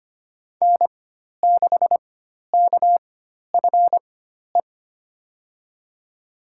Morse code, heard as a keyed tone, about 25 wpm, spaced as for 12 wpm, 700 Hz. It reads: N6KFE
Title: USA Call Signs - 5 Characters - 4x Character Spacing - 25wpm